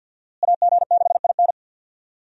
Morse code, heard as a keyed tone, about 47 wpm, 700 Hz. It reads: AG6IN
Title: USA Call Signs - 5 Characters - Repeated 3x - 50wpm